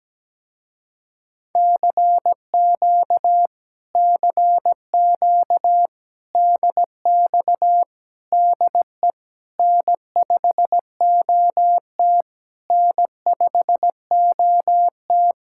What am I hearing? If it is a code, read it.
Morse code, heard as a keyed tone, about 17 wpm, 700 Hz. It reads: CQ CQ DX DE N5OT N5OT
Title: Speed-Racing Common QSO Phrases - 17wpm